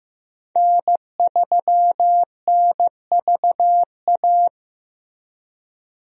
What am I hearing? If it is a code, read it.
Morse code, heard as a keyed tone, about 15 wpm, 700 Hz. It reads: N3NVA
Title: USA Call Signs - 5 Characters - Repeated 3x - 15wpm